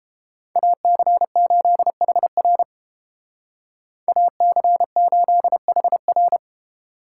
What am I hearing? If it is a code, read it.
Morse code, heard as a keyed tone, about 29 wpm, 700 Hz. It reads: AC8HR AC8HR